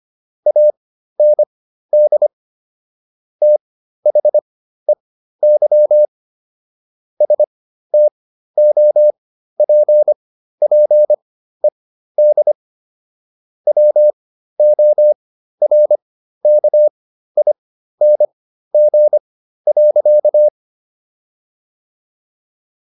Morse code, heard as a keyed tone, about 25 wpm, spaced as for 13 wpm, 600 Hz. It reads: AND THEY STOPPED WORKING.